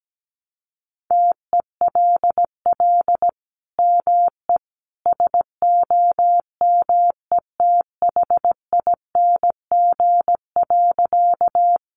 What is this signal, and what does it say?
Morse code, heard as a keyed tone, about 17 wpm, 700 Hz. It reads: TELL ME SOMETHING.
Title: Sentences from Top 600 Words 17wpm